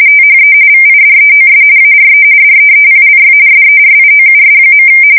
RTTY, about 45.45 baud. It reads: THE VIDEOS ARE DEEPLY SERIOUS.
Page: 170 Shift 2125/2295